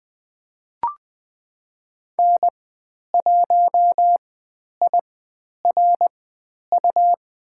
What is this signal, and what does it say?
Morse code, heard as a keyed tone, about 20 wpm, spaced as for 10 wpm, 700 Hz. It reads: N1IRU
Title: 102 Lesson 17g - Letter u - Call Signs 20wpm